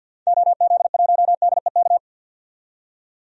Morse code, heard as a keyed tone, about 52 wpm, 700 Hz. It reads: OZ1DEK
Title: DXCC-100 Call Signs - 6 Characters - 50wpm